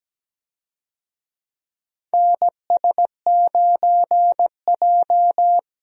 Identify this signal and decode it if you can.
Morse code, heard as a keyed tone, about 17 wpm, 700 Hz. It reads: NS9J
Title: USA Call Signs - 4 Characters - 17wpm